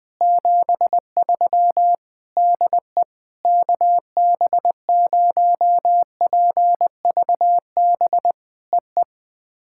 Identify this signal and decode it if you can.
Morse code, heard as a keyed tone, about 19 wpm, 700 Hz. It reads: 73 DE KB0PVB EE
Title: POTA QSO Practice - 20wpm